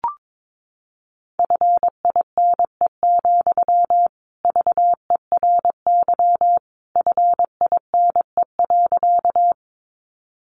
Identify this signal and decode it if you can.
Morse code, heard as a keyed tone, about 22 wpm, 700 Hz. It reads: FINE, VERY FINE.
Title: Sentences from Top 500 Words 22wpm